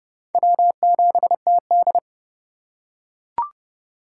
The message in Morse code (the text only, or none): W7TD